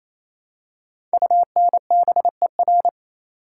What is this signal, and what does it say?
Morse code, heard as a keyed tone, about 28 wpm, 700 Hz. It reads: UNBER
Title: Code Groups - 5 Letters - 28wpm